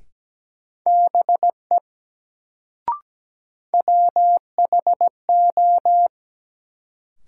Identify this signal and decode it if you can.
Morse code, heard as a keyed tone, about 17 wpm, 700 Hz. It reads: BE WHO
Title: Sentences from Top 100 Words - New Word Drill - 17wpm